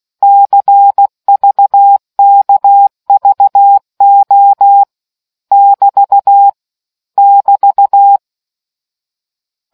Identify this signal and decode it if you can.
Morse code, heard as a keyed tone, about 16 wpm, 800 Hz. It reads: CVKVO <BT> <BT>